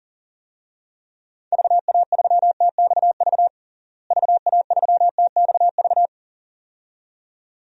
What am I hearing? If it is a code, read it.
Morse code, heard as a keyed tone, about 37 wpm, 700 Hz. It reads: VA3TXV VA3TXV
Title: DXCC-100 Call Signs - 6 Characters - Repeated 2x - 40wpm